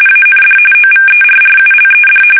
RTTY, about 45.45 baud. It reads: AND OVERRULED
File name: fsk850 / USS-Massachusetts